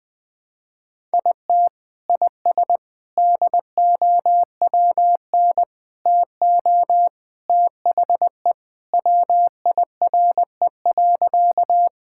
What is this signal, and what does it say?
Morse code, heard as a keyed tone, about 20 wpm, 700 Hz. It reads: IT IS DOWN TO THE WIRE.
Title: Sentences with 3-Word Phrases - Spoken Beginning - 20wpm